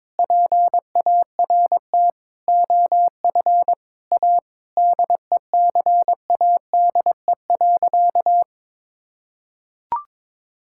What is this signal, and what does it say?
Morse code, heard as a keyed tone, about 21 wpm, 700 Hz. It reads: PART OF A DECADE.
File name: Sentences with 7-Word Phrases - 22wpm